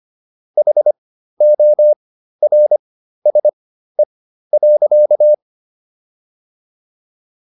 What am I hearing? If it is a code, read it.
Morse code, heard as a keyed tone, about 25 wpm, spaced as for 13 wpm, 600 Hz. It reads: HORSE.